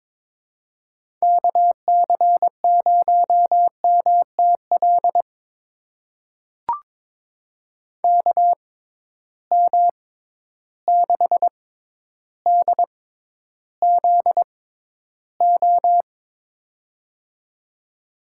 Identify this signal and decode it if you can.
Morse code, heard as a keyed tone, about 22 wpm, spaced as for 11 wpm, 700 Hz. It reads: KC0MTL KM6DZO
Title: USA Call Signs - 6 Characters - 6x Character Spacing - 22wpm